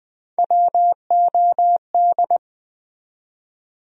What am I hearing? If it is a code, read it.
Morse code, heard as a keyed tone, about 20 wpm, 700 Hz. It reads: WOD
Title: USA Call Sign Postfixes - 3 Letters 20wpm